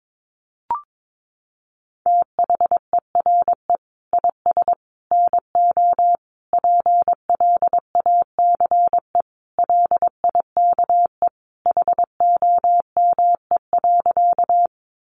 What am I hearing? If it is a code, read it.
Morse code, heard as a keyed tone, about 22 wpm, 700 Hz. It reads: THERE IS NO PLACE LIKE HOME.
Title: Sentences from Top 200 Words 22wpm